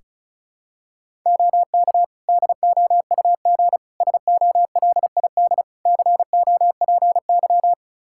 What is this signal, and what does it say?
Morse code, heard as a keyed tone, about 36 wpm, 700 Hz. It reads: OK DOUG SOLID COPY
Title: Practice QSO - Exchange 3 - More Info - 35wpm